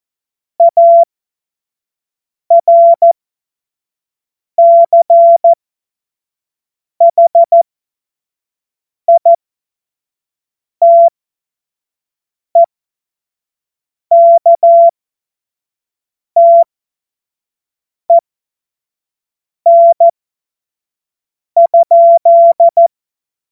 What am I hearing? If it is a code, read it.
Morse code, heard as a keyed tone, about 14 wpm, spaced as for 5 wpm, 680 Hz. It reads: ARCHITEKTEN?
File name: Morsekurs / Witze / Witz 14.November 2023 25 F70